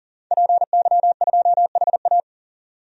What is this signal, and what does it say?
Morse code, heard as a keyed tone, about 40 wpm, 700 Hz. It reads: PY2HA
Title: DXCC-100 Call Signs - 5 Characters - 40wpm